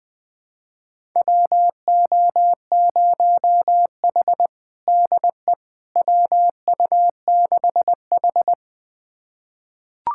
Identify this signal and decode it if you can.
Morse code, heard as a keyed tone, about 21 wpm, 700 Hz. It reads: WO0H DE WU6H
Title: USA Callsign DE Callsign - 4 Characters - 20wpm